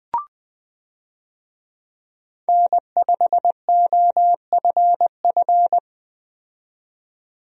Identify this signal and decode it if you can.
Morse code, heard as a keyed tone, about 21 wpm, 700 Hz. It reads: N5OFF
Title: USA Call Signs - 5 Characters - 20wpm